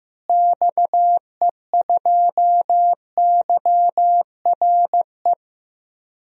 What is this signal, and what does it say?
Morse code, heard as a keyed tone, about 15 wpm, 700 Hz. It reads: XE2YRE
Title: DXCC-100 Call Signs - 6 Characters - Repeated 2x - 15wpm